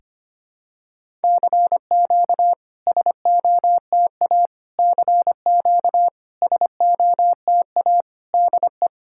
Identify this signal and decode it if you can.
Morse code, heard as a keyed tone, about 25 wpm, 690 Hz. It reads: CQ SOTA CQ SOTA DE
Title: SOTA QSO Practice 25wpm